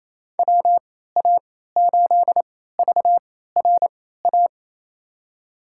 Morse code, heard as a keyed tone, about 28 wpm, spaced as for 16 wpm, 700 Hz. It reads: WA8VRA
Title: USA Call Signs - 6 Characters - 3x Character Spacing - 28wpm